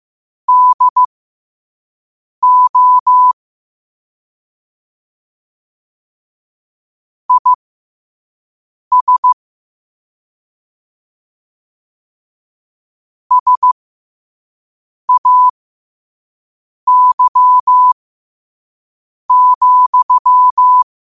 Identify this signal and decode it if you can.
Morse code, heard as a keyed tone, about 15 wpm, spaced as for 5 wpm, 1000 Hz. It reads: DO IS SAY,
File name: qotdfiles / 5wpm